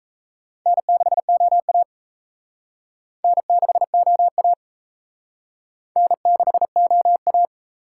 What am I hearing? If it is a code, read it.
Morse code, heard as a keyed tone, about 35 wpm, 700 Hz. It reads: N6OA N6OA N6OA
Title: Speed Racing - USA 1x2 Call Signs 28wpm